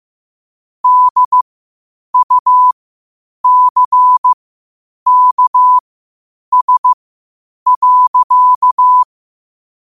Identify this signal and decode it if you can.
Morse code, heard as a keyed tone, about 15 wpm, spaced as for 8 wpm, 1000 Hz. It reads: DUCKS.
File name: qotdfiles / 8wpm